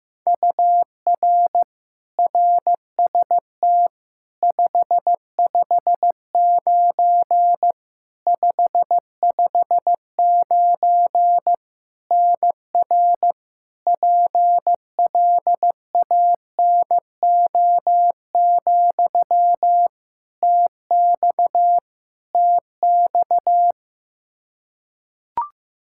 Morse code, heard as a keyed tone, about 15 wpm, 700 Hz. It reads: UR RST 559 559 NR PLANO, TX TX
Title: Practice QSO - Exchange 1 - Beginning - 15wpm